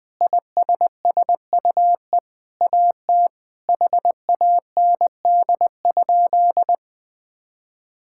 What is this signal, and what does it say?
Morse code, heard as a keyed tone, about 21 wpm, 700 Hz. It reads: ISSUE AT HAND?
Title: Sentences with 9-Word Phrases - Spoken Beginning - 20wpm